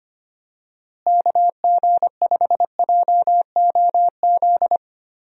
Morse code, heard as a keyed tone, about 25 wpm, 700 Hz. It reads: KG5JOZ
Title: USA Call Signs - 6 Characters - Repeated 2x - 25wpm